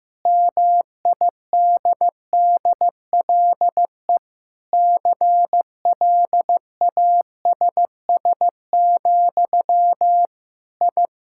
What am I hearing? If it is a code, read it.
Morse code, heard as a keyed tone, about 15 wpm, 700 Hz. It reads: MIDDLE CLASS, I
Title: Sentences from Top 1000 Words - Review Part 6 - 15wpm